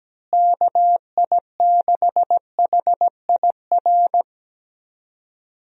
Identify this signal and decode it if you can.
Morse code, heard as a keyed tone, about 17 wpm, 700 Hz. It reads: KI6HIR
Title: USA Call Signs - 6 Characters - Repeated 3x - 17wpm